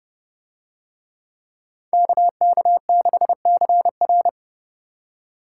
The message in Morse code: KK6CR

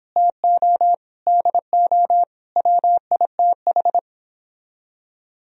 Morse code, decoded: TO DO WITH

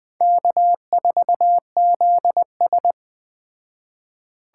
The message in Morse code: K4ZS

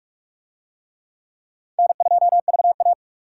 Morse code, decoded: N2VA

